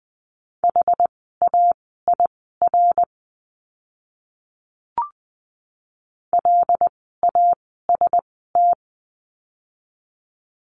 Morse code, heard as a keyed tone, about 20 wpm, spaced as for 14 wpm, 700 Hz. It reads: HAIR LAST
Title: 101 Lesson 17f - Letter u - Words 20wpm